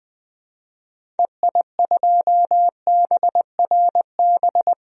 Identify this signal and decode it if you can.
Morse code, heard as a keyed tone, about 21 wpm, 700 Hz. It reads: EI2BRB
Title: DXCC-100 Call Signs - 6 Characters - 20wpm